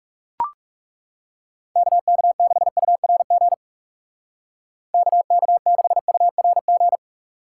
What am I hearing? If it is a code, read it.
Morse code, heard as a keyed tone, about 36 wpm, 700 Hz. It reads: KK6URG KK6URG
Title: Speed Racing - USA 2x3 Call Signs 30wpm